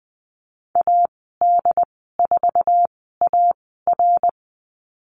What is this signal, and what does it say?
Morse code, heard as a keyed tone, about 19 wpm, spaced as for 14 wpm, 700 Hz. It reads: AD4AR
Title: 069 Lesson 12g - Letter d - Call Signs 20wpm